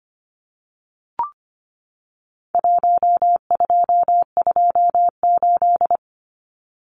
Morse code, heard as a keyed tone, about 25 wpm, 700 Hz. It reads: 1228
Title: Numbers - Contest Style - 4-Digit 25wpm